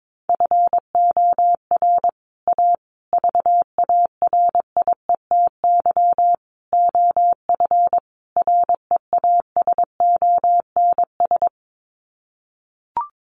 Morse code, heard as a keyed tone, about 22 wpm, 700 Hz. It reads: FOR A VARIETY OF REASONS
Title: Sets of 5 Words Encore 22wpm